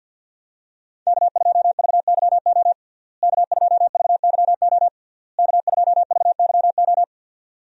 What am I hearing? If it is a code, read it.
Morse code, heard as a keyed tone, about 50 wpm, 700 Hz. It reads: K2VYO K2VYO K2VYO
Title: USA Call Signs - 5 Characters - Repeated 3x - 50wpm